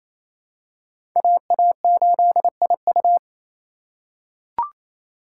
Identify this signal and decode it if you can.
Morse code, heard as a keyed tone, about 28 wpm, 700 Hz. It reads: AA8IU
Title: USA Call Signs - 2x Character Spacing - 28wpm